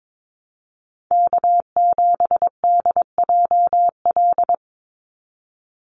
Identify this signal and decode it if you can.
Morse code, heard as a keyed tone, about 22 wpm, 700 Hz. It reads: K7DJL
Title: NA SOTA Chaser Callsigns 22wpm